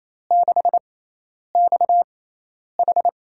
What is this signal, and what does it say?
Morse code, heard as a keyed tone, about 28 wpm, spaced as for 10 wpm, 700 Hz. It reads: 6XH
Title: USA Call Signs - 5 Characters - 6x Character Spacing - 28wpm